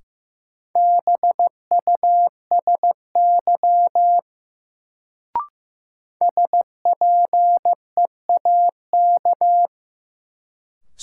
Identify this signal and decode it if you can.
Morse code, heard as a keyed tone, about 15 wpm, 700 Hz. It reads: BUSY SPEAK
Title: Sentences from Top 1000 Words - Review Part 2 - New Word Drill - 15wpm